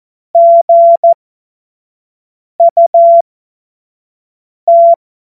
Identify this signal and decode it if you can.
Morse code, heard as a keyed tone, about 14 wpm, spaced as for 5 wpm, 680 Hz. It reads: GUT
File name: Morsekurs / Witze / Witz 09 April 2024 25 F70